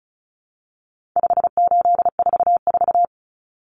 Morse code, heard as a keyed tone, about 34 wpm, 700 Hz. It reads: 5844
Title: Common Cut Numbers - 4-Digit 35wpm